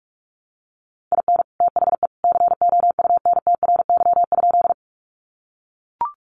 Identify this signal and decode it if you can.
Morse code, heard as a keyed tone, about 45 wpm, 700 Hz. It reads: IN THE COUNTRY?
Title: Sentences from Top 300 Words 45wpm